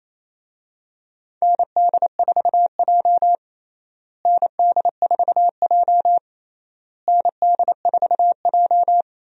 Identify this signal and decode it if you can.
Morse code, heard as a keyed tone, about 28 wpm, 700 Hz. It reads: ND4J ND4J ND4J
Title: USA Call Signs - 4 Characters - Repeated 3x - 28wpm